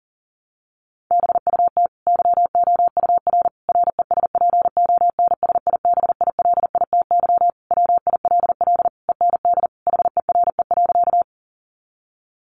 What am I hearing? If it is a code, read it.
Morse code, heard as a keyed tone, about 40 wpm, 700 Hz. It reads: BUT YOUR RESPONSIBILITY WILL END HERE.
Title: Sentences from Top 1200 Words 40wpm